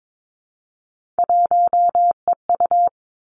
Morse code, heard as a keyed tone, about 22 wpm, 700 Hz. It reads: 1EU
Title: Three Challenging Letter-Number Combinations 22wpm